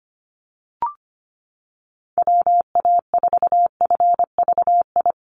Morse code, heard as a keyed tone, about 25 wpm, 700 Hz. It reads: WA4FVI
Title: USA Call Signs 25wpm